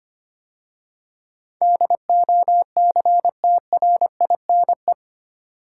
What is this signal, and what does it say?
Morse code, heard as a keyed tone, about 25 wpm, 690 Hz. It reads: DOCTRINE
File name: 8 Letter Words - First 3 Letters Given - 25wpm